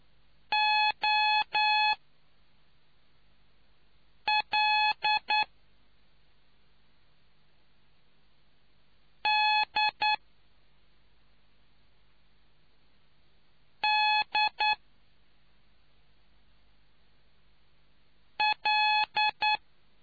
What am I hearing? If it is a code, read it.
Morse code, heard as a keyed tone, about 9 wpm, spaced as for 3 wpm, 850 Hz. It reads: OL D D L